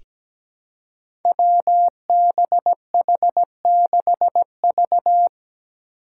Morse code, heard as a keyed tone, about 17 wpm, 700 Hz. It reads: WBH6V
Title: Five Challenging Letter-Number Combinations 17wpm